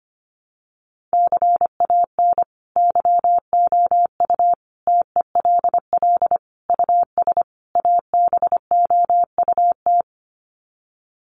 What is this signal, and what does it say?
Morse code, heard as a keyed tone, about 25 wpm, 700 Hz. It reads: CAN YOU TELL US ABOUT
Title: Sets of 5 Words Encore2 25wpm